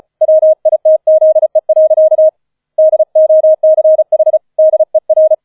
Morse code, heard as a keyed tone, about 34 wpm, 630 Hz. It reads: WITZE. DOCH DER